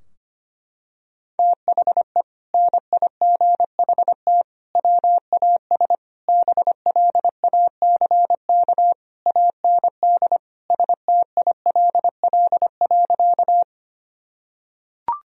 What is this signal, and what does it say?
Morse code, heard as a keyed tone, about 25 wpm, 700 Hz. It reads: THE NIGHT WAS BLACK AND STILL.
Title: Sentences from Top 1000 Words - Review Part 2 - 25wpm